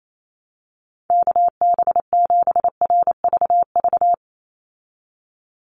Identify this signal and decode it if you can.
Morse code, heard as a keyed tone, about 28 wpm, 700 Hz. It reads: KB7RVV
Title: USA Call Signs 28wpm